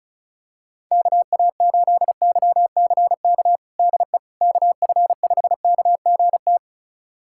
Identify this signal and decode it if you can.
Morse code, heard as a keyed tone, about 36 wpm, 700 Hz. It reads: KA8YCK DE KF5KGT
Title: USA Callsign DE Callsign - 6 Characters - 35wpm